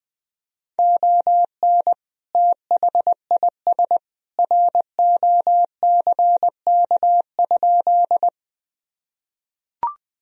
Morse code, heard as a keyed tone, about 20 wpm, 700 Hz. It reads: ON THIS ROCK?
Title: Sentences from Top 1000 Words - Review Part 2 - 20wpm